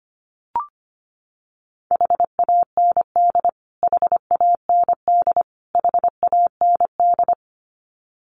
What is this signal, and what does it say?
Morse code, heard as a keyed tone, about 25 wpm, 700 Hz. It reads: HAND HAND HAND
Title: Top 500 Words Repeated 25wpm